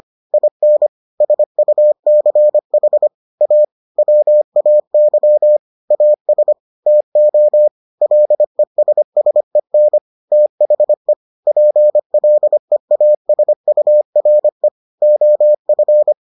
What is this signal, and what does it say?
Morse code, heard as a keyed tone, about 25 wpm, 600 Hz. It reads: IN SUCH A WAY AS TO LESSEN THE PLEASURE OF